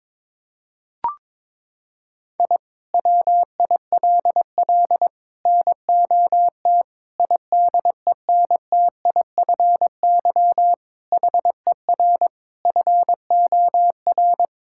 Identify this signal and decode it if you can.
Morse code, heard as a keyed tone, about 22 wpm, 700 Hz. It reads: I WILL NOT IDENTIFY HER FOR
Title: Sentences from Top 1000 Words - Review Part 6 - 22wpm